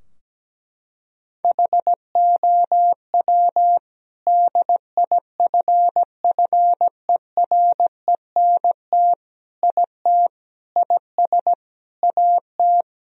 Morse code, heard as a keyed tone, about 17 wpm, 700 Hz. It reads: HOW DIFFERENT IT IS AT
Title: Sentences from Top 1000 Words - Review Part 3 - 17wpm